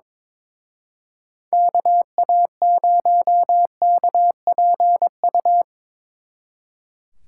Morse code, heard as a keyed tone, about 24 wpm, 700 Hz. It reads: KA0KPU
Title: Speed Racing - USA 2x3 Call Signs 22wpm